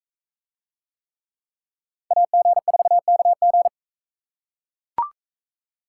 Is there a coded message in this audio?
Yes: Morse code from 2.1 to 3.7 s, heard as a keyed tone, about 40 wpm, 700 Hz. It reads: AG4KG